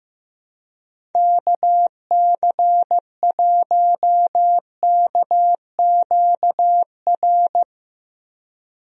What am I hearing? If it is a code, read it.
Morse code, heard as a keyed tone, about 15 wpm, 700 Hz. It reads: KC1KQR
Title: USA Call Signs - 6 Characters - Repeated 2x - 15wpm